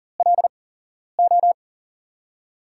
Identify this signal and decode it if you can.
Morse code, heard as a keyed tone, about 40 wpm, spaced as for 11 wpm, 700 Hz. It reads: LO
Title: USA Call Signs - 6 Characters - 8x Character Spacing - 40wpm